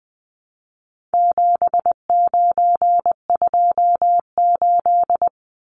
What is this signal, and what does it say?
Morse code, heard as a keyed tone, about 20 wpm, 700 Hz. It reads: 7928
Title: Numbers - 4-Digit 20wpm